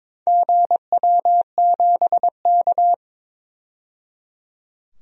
Morse code, heard as a keyed tone, about 23 wpm, 700 Hz. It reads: GW7K
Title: DXCC-100 Call Signs - 4 Characters - 22wpm